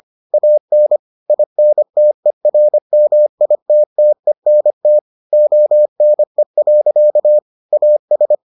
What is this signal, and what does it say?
Morse code, heard as a keyed tone, about 25 wpm, 600 Hz. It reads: AN INTERMITTENT ONE. AS